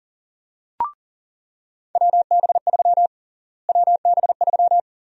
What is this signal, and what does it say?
Morse code, heard as a keyed tone, about 40 wpm, 700 Hz. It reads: WB3 WB3
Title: USA Call Sign Prefixes - Repeated 2x - 40wpm